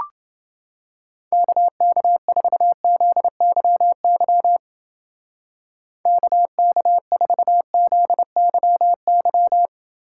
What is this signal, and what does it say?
Morse code, heard as a keyed tone, about 29 wpm, 700 Hz. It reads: KK4ZYY KK4ZYY